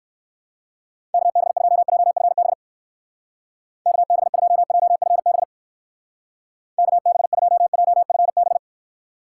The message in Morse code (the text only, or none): KB2JFB KB2JFB KB2JFB